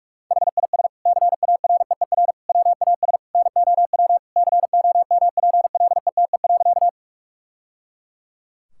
Morse code, heard as a keyed tone, about 46 wpm, 700 Hz. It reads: HIS CAREER WAS NOW COMPLETE.
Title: Sentences from Top 1000 Words - Review Part 6 - 45wpm